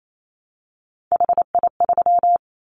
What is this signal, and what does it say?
Morse code, heard as a keyed tone, about 28 wpm, 700 Hz. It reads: HI3
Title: Callsign Prefixes for Top 100 DXCC - 28wpm